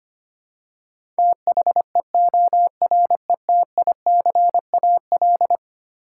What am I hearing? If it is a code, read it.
Morse code, heard as a keyed tone, about 25 wpm, 700 Hz. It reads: THEORETICAL
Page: morse-code-ninja-practice-25wpm